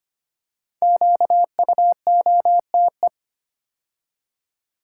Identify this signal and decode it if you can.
Morse code, heard as a keyed tone, about 25 wpm, 700 Hz. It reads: QUOTE
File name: 5 Letter Words - First 3 Letters Given - 25wpm